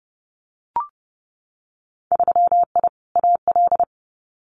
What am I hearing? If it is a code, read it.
Morse code, heard as a keyed tone, about 30 wpm, 700 Hz. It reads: 3I AL
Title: Winter Field Day Exchanges Encore 30wpm